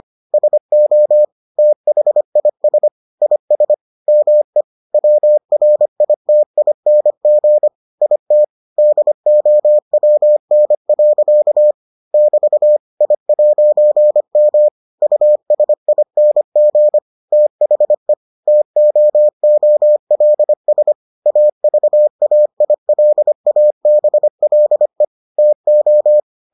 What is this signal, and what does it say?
Morse code, heard as a keyed tone, about 25 wpm, 600 Hz. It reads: SO THIS IS ME WRITING IT DOWN. <BT> I'M USING THE TOOLS AVAILABLE TO